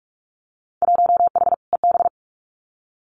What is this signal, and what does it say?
Morse code, heard as a keyed tone, about 46 wpm, 700 Hz. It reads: 1H EB
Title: Winter Field Day Exchanges 45wpm